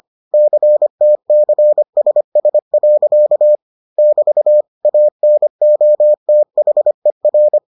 Morse code, heard as a keyed tone, about 25 wpm, 600 Hz. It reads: CTCSS. <BT> ANOTHER